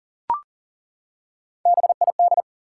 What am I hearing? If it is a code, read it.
Morse code, heard as a keyed tone, about 40 wpm, 700 Hz. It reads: BID
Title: Three Letter Words - 40wpm